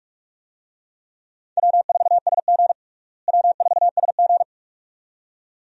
Morse code, heard as a keyed tone, about 46 wpm, 700 Hz. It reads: W4SG W4SG
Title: USA Call Signs - 4 Characters - Repeated 2x - 45wpm